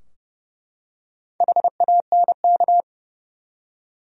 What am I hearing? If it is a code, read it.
Morse code, heard as a keyed tone, about 30 wpm, 700 Hz. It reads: HANK
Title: USA Common Names - Max 5 Letters - 30wpm